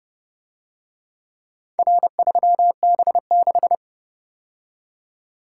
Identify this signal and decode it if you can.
Morse code, heard as a keyed tone, about 29 wpm, 700 Hz. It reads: R3B6